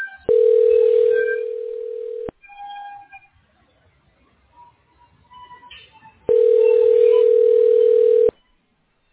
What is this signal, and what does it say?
Ringback tone (North America), 440 + 480 Hz.